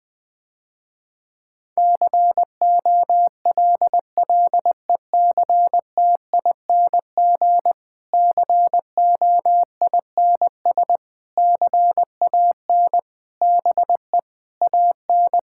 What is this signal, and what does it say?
Morse code, heard as a keyed tone, about 21 wpm, 700 Hz. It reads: COLLECTING COINS CAN BE AN
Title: Difficult Sentences with Hints - 20wpm